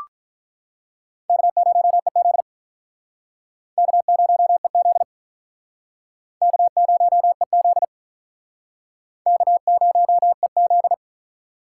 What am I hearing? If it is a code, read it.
Morse code, heard as a keyed tone, about 47 wpm, 700 Hz. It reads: K0EZ K0EZ K0EZ K0EZ